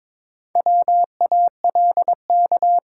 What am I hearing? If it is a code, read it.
Morse code, heard as a keyed tone, about 22 wpm, 700 Hz. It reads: WALK
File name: practice / 22wpm / Sentences with 2-Word Phrases - 22wpm